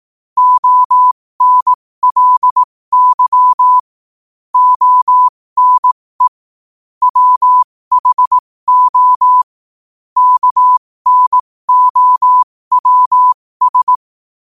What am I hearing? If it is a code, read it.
Morse code, heard as a keyed tone, about 18 wpm, spaced as for 15 wpm, 1000 Hz. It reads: ONLY ONE WHO KNOWS
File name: qotdfiles / 15wpm